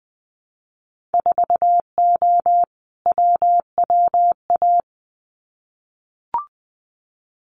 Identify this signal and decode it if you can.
Morse code, heard as a keyed tone, about 20 wpm, 700 Hz. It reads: 4O WWA
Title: Winter Field Day Exchanges 20wpm